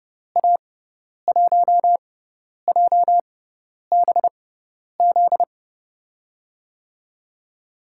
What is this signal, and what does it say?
Morse code, heard as a keyed tone, about 32 wpm, spaced as for 10 wpm, 700 Hz. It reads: A1JBZ
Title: USA Call Signs - 6 Characters - 6x Character Spacing - 30wpm